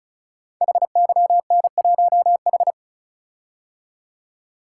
Morse code, heard as a keyed tone, about 36 wpm, 700 Hz. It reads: HYN1H
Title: Code Groups - 5 Letter:Numbers - 35wpm